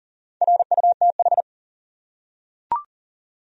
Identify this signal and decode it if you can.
Morse code, heard as a keyed tone, about 40 wpm, 700 Hz. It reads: RUTH
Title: USA Common Names - Max 5 Letters - 40wpm